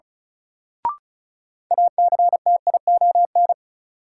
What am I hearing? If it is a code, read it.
Morse code, heard as a keyed tone, about 36 wpm, 700 Hz. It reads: ACTION
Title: Sentences from Top 700 Words - New Word Drill - 35wpm